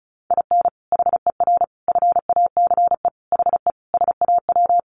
Morse code, heard as a keyed tone, about 34 wpm, 700 Hz. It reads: IN HER FACE HE SAW
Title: Sentences from Top 1300 Words 35wpm